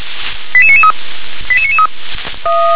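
Five-tone selective call (ZVEI), heard as quick runs of five tones, tones of about 80 ms. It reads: 88993 88993
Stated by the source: ZVEI five-tone selective call received on an SDR.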